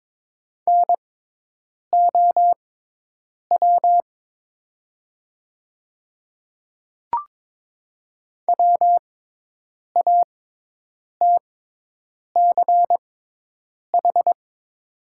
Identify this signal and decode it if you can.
Morse code, heard as a keyed tone, about 22 wpm, spaced as for 8 wpm, 700 Hz. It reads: NOW WATCH
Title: 6x Character Spacing - Top 500 Words - 22wpm